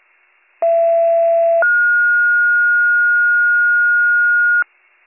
Two-tone page, heard as a one-second tone, then a three-second tone, about 670 then 1430 Hz. It is heard from 0.6 to 4.6 s.